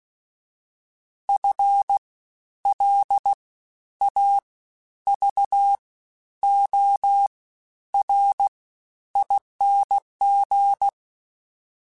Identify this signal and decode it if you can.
Morse code, heard as a keyed tone, about 16 wpm, 800 Hz. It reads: F L A V O R ING
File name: Special INGs-mp3s / Special INGs-mp3s / Fast ing @15